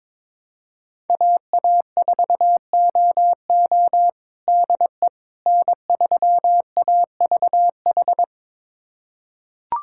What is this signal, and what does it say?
Morse code, heard as a keyed tone, about 22 wpm, 700 Hz. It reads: AA4OO DE N3AVH
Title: USA Callsign DE Callsign - 5 Characters - 22wpm